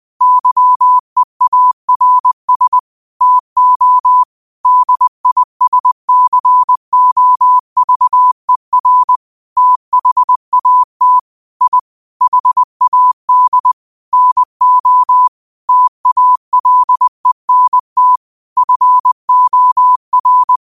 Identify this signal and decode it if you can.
Morse code, heard as a keyed tone, about 20 wpm, 1000 Hz. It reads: YEARS TO DISCOVER THAT I HAD NO TALENT FOR
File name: qotdfiles / 20wpm